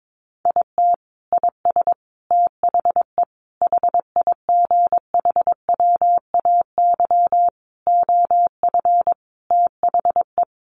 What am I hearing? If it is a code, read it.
Morse code, heard as a keyed tone, about 22 wpm, 700 Hz. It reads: IT IS THE HIGHWAY OF THE
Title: Sentences from Top 1400 Words 22wpm